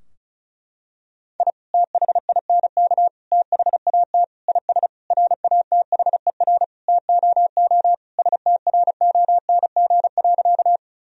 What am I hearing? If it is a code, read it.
Morse code, heard as a keyed tone, about 36 wpm, 700 Hz. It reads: I THINK THAT IS RATHER TOO STRONG.
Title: Sentences from Top 1000 Words - Review Part 4 - 35wpm